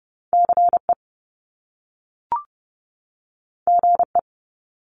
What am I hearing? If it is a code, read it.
Morse code, heard as a keyed tone, about 30 wpm, 700 Hz. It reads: CE GE
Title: Top Two Letter Combinations in Words 30wpm